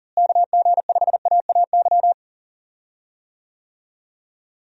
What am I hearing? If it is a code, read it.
Morse code, heard as a keyed tone, about 40 wpm, 700 Hz. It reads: KG5AAY
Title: USA Call Signs - 6 Characters - Repeated 3x - 40wpm